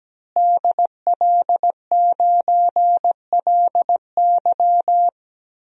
Morse code, heard as a keyed tone, about 17 wpm, 700 Hz. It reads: DL9LY
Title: DXCC-100 Call Signs - 5 Characters - Repeated 2x - 17wpm